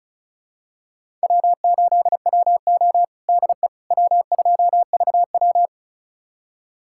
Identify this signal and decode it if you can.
Morse code, heard as a keyed tone, about 36 wpm, 700 Hz. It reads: W8WO DE W2VW
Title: USA Callsign DE Callsign - 4 Characters - 35wpm